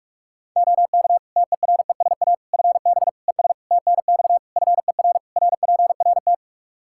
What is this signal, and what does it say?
Morse code, heard as a keyed tone, about 46 wpm, 700 Hz. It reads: OK TERESA FB ES TNX FER RPRT
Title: Practice QSO - Exchange 2 - Info - 45wpm